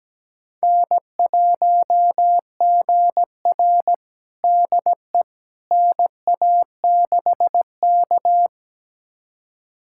Morse code, heard as a keyed tone, about 17 wpm, 700 Hz. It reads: N1GR DE NA6K
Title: USA Callsign DE Callsign - 4 Characters - 17wpm